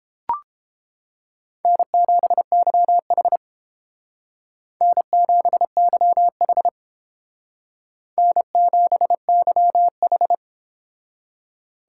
Morse code, heard as a keyed tone, about 30 wpm, 700 Hz. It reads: N7YH N7YH N7YH